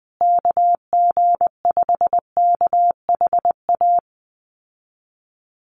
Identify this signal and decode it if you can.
Morse code, heard as a keyed tone, about 19 wpm, 700 Hz. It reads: KG5KHA
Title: USA Call Signs Repeated 3x 20wpm